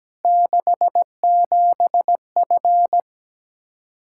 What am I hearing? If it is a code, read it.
Morse code, heard as a keyed tone, about 17 wpm, 700 Hz. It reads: 67F
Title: K1USN SST QSO Practice 17wpm